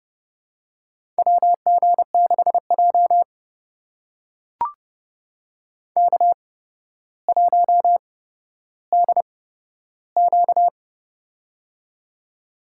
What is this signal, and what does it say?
Morse code, heard as a keyed tone, about 30 wpm, spaced as for 13 wpm, 700 Hz. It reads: WG6J K1DQ
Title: USA Call Signs - 4 Characters - 8x Character Spacing - 30wpm